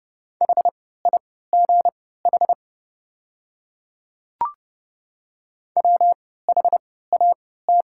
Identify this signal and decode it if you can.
Morse code, heard as a keyed tone, about 30 wpm, spaced as for 17 wpm, 690 Hz. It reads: HIGH WHAT
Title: 3x Character Spacing - Top 200 Words - 30wpm